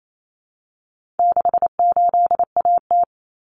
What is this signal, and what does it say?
Morse code, heard as a keyed tone, about 28 wpm, 700 Hz. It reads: 68AT
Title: Common Cut Numbers - 4-Digit 28wpm